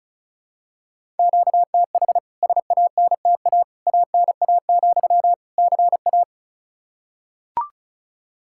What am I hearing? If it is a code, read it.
Morse code, heard as a keyed tone, about 36 wpm, 700 Hz. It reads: QTH SANTA ANA, CA
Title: K1USN SST QSO Practice 35wpm